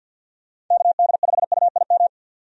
Morse code, heard as a keyed tone, about 49 wpm, 700 Hz. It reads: KD5UIM
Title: USA Call Signs - 6 Characters - 50wpm